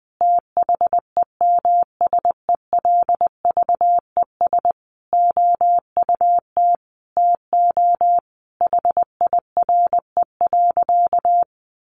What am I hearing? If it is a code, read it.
Morse code, heard as a keyed tone, about 19 wpm, 700 Hz. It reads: THEMSELVES OUT TO HIRE.
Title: Sentences from Top 1400 Words 20wpm